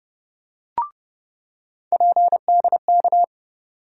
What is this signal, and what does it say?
Morse code, heard as a keyed tone, about 30 wpm, 700 Hz. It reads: PDK
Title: USA Call Sign Postfixes - 3 Letters 30wpm